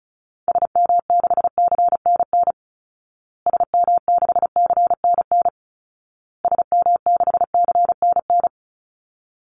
Morse code, heard as a keyed tone, about 34 wpm, 700 Hz. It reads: SM6CNN SM6CNN SM6CNN
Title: Call Signs Repeated 2x - 35wpm